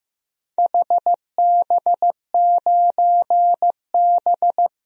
Morse code, heard as a keyed tone, about 15 wpm, 700 Hz. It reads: HB9B
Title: DXCC-100 Call Signs - 4 Characters - 15wpm